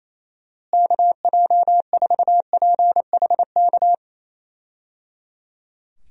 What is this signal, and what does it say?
Morse code, heard as a keyed tone, about 28 wpm, 700 Hz. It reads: KJ4PHK
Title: USA Call Signs - 6 Characters - Repeated 3x - 28wpm